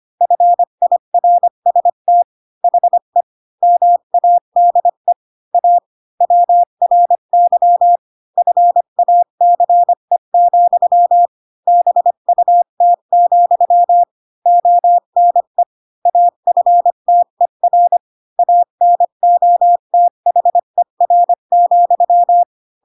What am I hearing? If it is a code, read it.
Morse code, heard as a keyed tone, about 25 wpm, 700 Hz. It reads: FIRST HE MADE A WRY FACE, BUT, ONE AFTER ANOTHER,
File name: Pinocchio / Pinocchio ch7 20 25 ik1hge